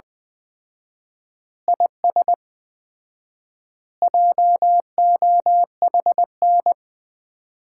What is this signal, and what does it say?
Morse code, heard as a keyed tone, about 20 wpm, 700 Hz. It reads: IS JOHN
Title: Common QSO Phrases - 4x Word-Spacing - 20wpm